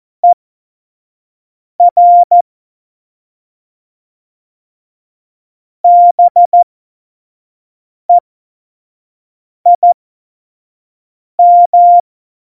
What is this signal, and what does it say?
Morse code, heard as a keyed tone, about 14 wpm, spaced as for 5 wpm, 700 Hz. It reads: ER BEIM
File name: Morsekurs / Witze / Witz 14 Juli 2020, 25 F70